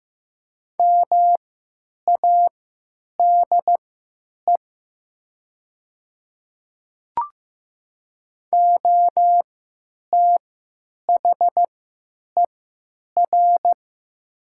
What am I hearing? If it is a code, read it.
Morse code, heard as a keyed tone, about 15 wpm, spaced as for 9 wpm, 700 Hz. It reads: MADE OTHER
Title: 3x Character Spacing - Top 200 Words - 15wpm